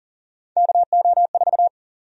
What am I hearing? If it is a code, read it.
Morse code, heard as a keyed tone, about 38 wpm, 700 Hz. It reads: KO4
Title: USA Call Sign Prefixes - Repeated 2x - 40wpm